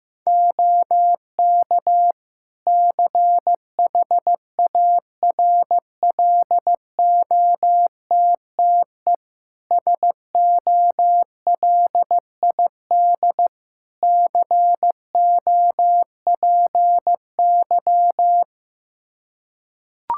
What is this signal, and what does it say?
Morse code, heard as a keyed tone, about 15 wpm, 700 Hz. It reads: OK CHARLOTTE SOLID COPY
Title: Practice QSO - Exchange 3 - More Info - 15wpm